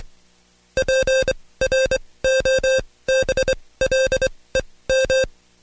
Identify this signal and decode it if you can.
Morse code, heard as a keyed tone, about 24 wpm, spaced as for 18 wpm, 530 Hz. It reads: PROBLEM
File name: Code Quick / AudioDownld2 / Seg135-Stepper8 17WPM